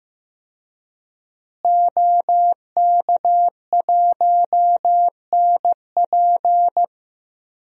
Morse code, heard as a keyed tone, about 15 wpm, 700 Hz. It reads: OK1NP
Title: DXCC-100 Call Signs - 5 Characters - Repeated 3x - 15wpm